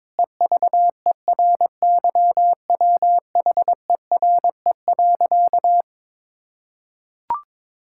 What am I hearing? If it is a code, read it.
Morse code, heard as a keyed tone, about 22 wpm, 700 Hz. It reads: EVERYWHERE.
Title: Sentences from Top 1600 Words 22wpm